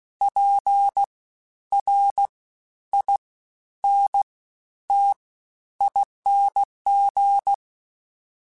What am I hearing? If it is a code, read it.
Morse code, heard as a keyed tone, about 16 wpm, spaced as for 10 wpm, 800 Hz. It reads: PRINTING
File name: Special INGs-mp3s / Special INGs-mp3s / SlowToFaster ing @15